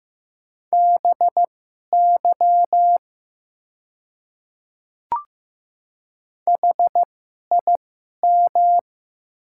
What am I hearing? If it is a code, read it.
Morse code, heard as a keyed tone, about 15 wpm, spaced as for 11 wpm, 700 Hz. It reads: BY HIM
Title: 2x Character Spacing - Top 100 Words - 15wpm